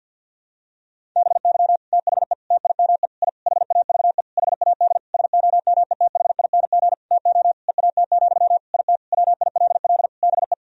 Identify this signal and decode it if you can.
Morse code, heard as a keyed tone, about 52 wpm, 700 Hz. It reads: BY THE TIME I HAVE HAD SOMETHING TO EAT, IT WILL BE
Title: Sentences from Top 1000 Words - Review Part 1 - 50wpm